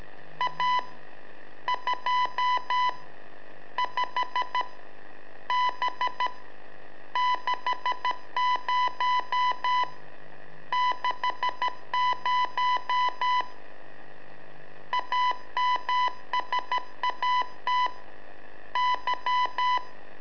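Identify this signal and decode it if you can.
Morse code, heard as a keyed tone, about 13 wpm, 1000 Hz. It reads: A 2 5 B 60 60 AMSAT Y
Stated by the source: Telemetry in Morse from LU7AA and LU4EZ, AMSAT and Turma Mar del Plata, sent from a balloon.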